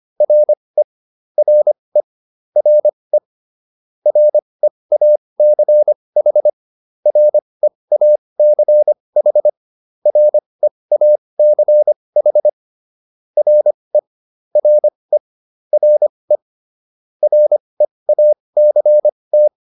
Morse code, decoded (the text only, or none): RE RE RE REACH REACH REACH RE RE RE REACT